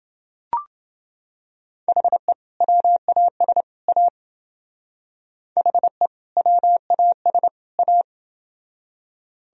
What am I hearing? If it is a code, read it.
Morse code, heard as a keyed tone, about 28 wpm, 700 Hz. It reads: HE WAS A HE WAS A